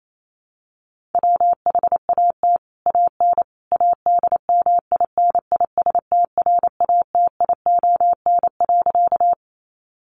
Morse code, heard as a keyed tone, about 28 wpm, 700 Hz. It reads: WHAT AN ADMINISTRATION.
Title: Sentences from Top 1300 Words 28wpm